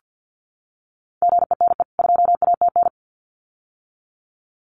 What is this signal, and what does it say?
Morse code, heard as a keyed tone, about 50 wpm, 700 Hz. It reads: GENE 2ATN
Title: CWT Exchange 50wpm